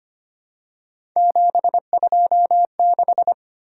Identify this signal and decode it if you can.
Morse code, heard as a keyed tone, about 25 wpm, 700 Hz. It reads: 726
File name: Three Letter-Number Combinations - 25wpm